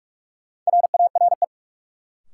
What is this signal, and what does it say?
Morse code, heard as a keyed tone, about 44 wpm, 700 Hz. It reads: RARE
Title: Sentences from Top 1100 Words - New Word Drill Alt - 45wpm